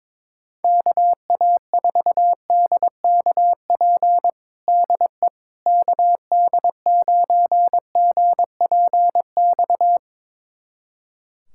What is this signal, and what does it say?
Morse code, heard as a keyed tone, about 22 wpm, 700 Hz. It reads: KA4DKP DE KD9GPX
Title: USA Callsign DE Callsign 22wpm